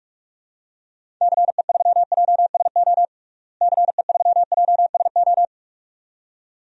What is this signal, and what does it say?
Morse code, heard as a keyed tone, about 41 wpm, 700 Hz. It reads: CE3JSO CE3JSO
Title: DXCC-100 Call Signs - 6 Characters - Repeated 2x - 45wpm